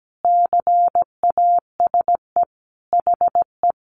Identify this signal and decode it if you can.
Morse code, heard as a keyed tone, about 17 wpm, 700 Hz. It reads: CASE HE
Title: Sentences from Top 1300 Words 17wpm